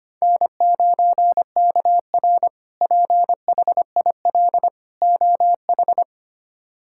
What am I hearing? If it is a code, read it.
Morse code, heard as a keyed tone, about 25 wpm, 700 Hz. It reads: N9KR PHIL OH
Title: K1USN SST QSO Practice - 25wpm